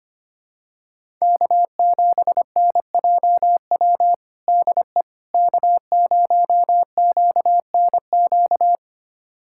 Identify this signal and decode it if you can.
Morse code, heard as a keyed tone, about 25 wpm, 700 Hz. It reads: K7NJW DE K0QNQ
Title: USA Callsign DE Callsign - 5 Characters - 25wpm